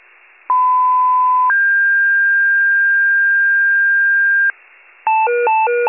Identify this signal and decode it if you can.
Two-tone page, heard as a one-second tone, then a three-second tone, about 1010 then 1600 Hz. It is heard from 0.5 to 4.5 s.
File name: pager / 94 RESCUE DUTY / wcalert